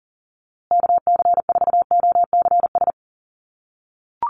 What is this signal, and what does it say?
Morse code, heard as a keyed tone, about 40 wpm, 700 Hz. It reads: KC4OCS
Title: USA Call Signs Repeated 2x 40wpm